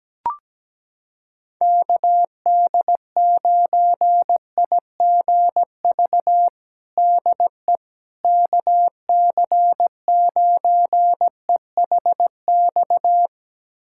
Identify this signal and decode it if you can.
Morse code, heard as a keyed tone, about 17 wpm, 700 Hz. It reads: KD9IGV DE KC9EHX
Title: USA Callsign DE Callsign - 6 Characters - 17wpm